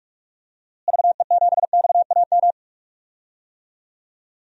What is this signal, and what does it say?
Morse code, heard as a keyed tone, about 46 wpm, 700 Hz. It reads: VE7XAM
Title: DXCC-100 Call Signs - 6 Characters - Repeated 3x - 45wpm